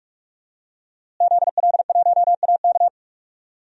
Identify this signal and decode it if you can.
Morse code, heard as a keyed tone, about 46 wpm, 700 Hz. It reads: ZL1AK
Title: DXCC-100 Call Signs - 5 Characters - Repeated 3x - 45wpm